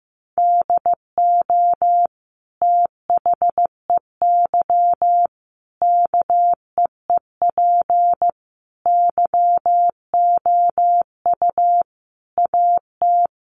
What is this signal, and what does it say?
Morse code, heard as a keyed tone, about 15 wpm, 700 Hz. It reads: DO THEY KEEP YOU AT
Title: Sentences from Top 1200 Words 15wpm